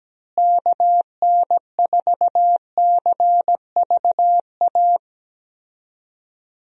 Morse code, heard as a keyed tone, about 17 wpm, 700 Hz. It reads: KN4CVA
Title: USA Call Signs - 6 Characters - Repeated 2x - 17wpm